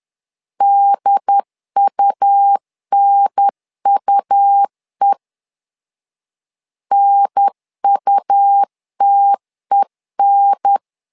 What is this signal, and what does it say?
Morse code, heard as a keyed tone, about 10 wpm, 800 Hz. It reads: DUNUE NUTEN
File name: LEZIONI / 2 UHDN / 2 UHDN 10wpm 1904